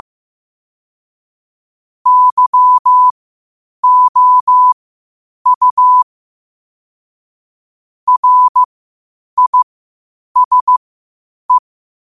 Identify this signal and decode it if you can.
Morse code, heard as a keyed tone, about 15 wpm, spaced as for 8 wpm, 1000 Hz. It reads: YOU RISE